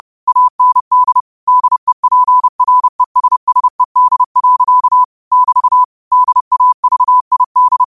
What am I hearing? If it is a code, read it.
Morse code, heard as a keyed tone, about 30 wpm, 1000 Hz. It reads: AND DEPRESSED. <BT> DAVID
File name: qotdfiles / 30wpm